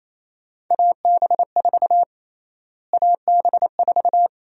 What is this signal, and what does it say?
Morse code, heard as a keyed tone, about 28 wpm, 700 Hz. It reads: AB4 AB4
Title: USA Call Sign Prefixes - Repeated 2x - 28wpm